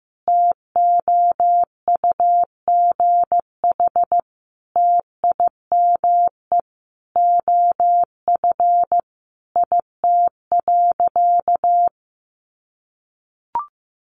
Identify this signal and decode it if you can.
Morse code, heard as a keyed tone, about 15 wpm, 700 Hz. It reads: TOUGH TIME OF IT.
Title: Sentences from Top 1200 Words 15wpm